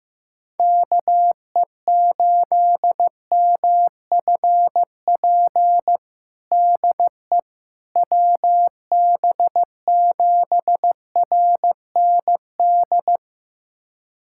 Morse code, heard as a keyed tone, about 15 wpm, 700 Hz. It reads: KE8MFP DE WB7RND
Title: USA Callsign DE Callsign - 6 Characters - 15wpm